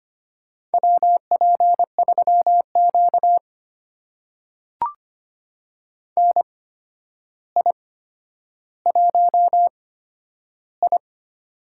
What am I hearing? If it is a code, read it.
Morse code, heard as a keyed tone, about 25 wpm, spaced as for 11 wpm, 700 Hz. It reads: WP3Q NI1I
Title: USA Call Signs - 4 Characters - 8x Character Spacing - 25wpm